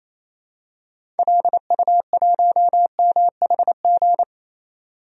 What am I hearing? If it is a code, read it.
Morse code, heard as a keyed tone, about 28 wpm, 700 Hz. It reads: LU1MHG
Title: DXCC-100 Call Signs - 6 Characters - 28wpm